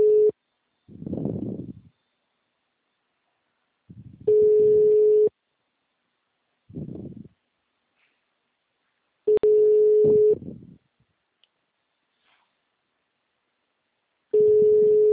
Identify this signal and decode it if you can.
Ringback tone (Europe, 425 Hz).